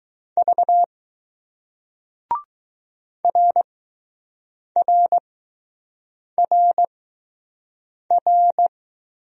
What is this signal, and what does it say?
Morse code, heard as a keyed tone, about 20 wpm, 700 Hz. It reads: V R R R R